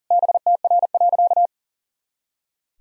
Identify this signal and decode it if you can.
Morse code, heard as a keyed tone, about 37 wpm, 700 Hz. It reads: BTR.
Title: Sentences from Top 400 Words and Abbreviations 40wpm